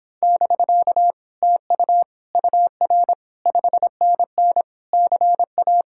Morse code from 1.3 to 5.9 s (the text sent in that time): TU UR 5NN CA